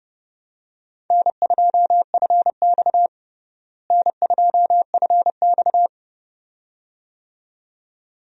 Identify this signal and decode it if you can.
Morse code, heard as a keyed tone, about 30 wpm, 700 Hz. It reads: N2FX N2FX
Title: USA Call Signs - 4 Characters - Repeated 2x - 30wpm